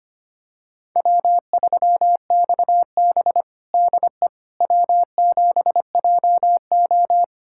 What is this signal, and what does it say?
Morse code, heard as a keyed tone, about 25 wpm, 700 Hz. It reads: W3XB DE W7JO
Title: USA Callsign DE Callsign - 4 Characters - 25wpm